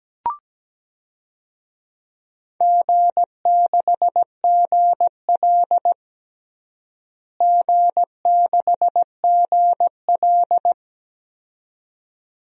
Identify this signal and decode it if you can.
Morse code, heard as a keyed tone, about 17 wpm, 700 Hz. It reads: G6GL G6GL
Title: DXCC-100 Call Signs - 4 Characters - Repeated 2x - 17wpm